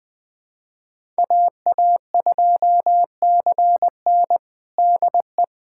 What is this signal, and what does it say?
Morse code, heard as a keyed tone, about 20 wpm, 700 Hz. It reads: AA2CN DE
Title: USA Callsign DE Callsign - 5 Characters - 20wpm